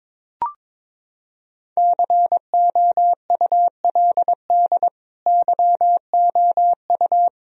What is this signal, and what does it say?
Morse code, heard as a keyed tone, about 22 wpm, 700 Hz. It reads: COULD YOU
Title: Sentences from Top 1000 Words - Review Part 1 - 22wpm